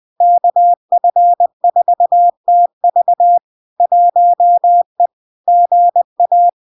Morse code, heard as a keyed tone, about 20 wpm, 700 Hz. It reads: KF4TV 1E GA
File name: LARK-FD-2022-Log-in-CW-20wpm-700-Hz